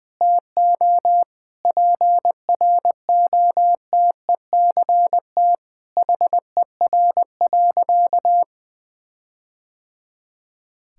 Morse code, heard as a keyed tone, about 20 wpm, 700 Hz. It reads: TO PROTECT HER.
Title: Sentences from Top 1600 Words 20wpm